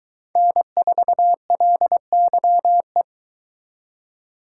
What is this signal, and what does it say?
Morse code, heard as a keyed tone, about 23 wpm, 700 Hz. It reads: N4LYE